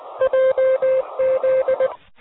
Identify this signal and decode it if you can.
Morse code, heard as a keyed tone, about 19 wpm, 520 Hz. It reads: JZ
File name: 04-06-26-j79kv-gw3mfy-loud